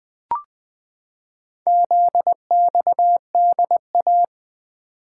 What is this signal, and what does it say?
Morse code, heard as a keyed tone, about 20 wpm, 700 Hz. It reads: ZXDA
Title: Four Letter-Number Combinations - 20wpm